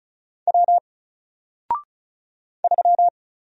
Morse code, heard as a keyed tone, about 36 wpm, 700 Hz. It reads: W 3
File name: Single-Letter-Number-35wpm